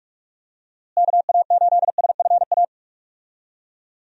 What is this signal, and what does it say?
Morse code, heard as a keyed tone, about 46 wpm, 700 Hz. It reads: KA8SFA
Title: USA Call Signs - 6 Characters - Repeated 2x - 45wpm